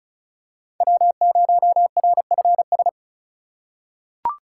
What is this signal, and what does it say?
Morse code, heard as a keyed tone, about 36 wpm, 700 Hz. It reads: W0RFS
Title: USA Call Signs - 3x Character Spacing - 35wpm